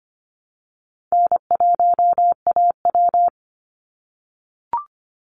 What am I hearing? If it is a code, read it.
Morse code, heard as a keyed tone, about 25 wpm, 700 Hz. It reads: N1AW
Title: NA SOTA Chaser Callsigns 25wpm